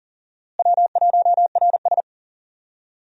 Morse code, heard as a keyed tone, about 39 wpm, 700 Hz. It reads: W1RS
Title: USA Call Signs - 4 Characters - Repeated 3x - 40wpm